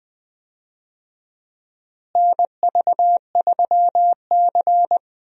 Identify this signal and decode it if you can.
Morse code, heard as a keyed tone, about 20 wpm, 700 Hz. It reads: NV3C
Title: USA Call Signs - 4 Characters - Repeated 2x - 20wpm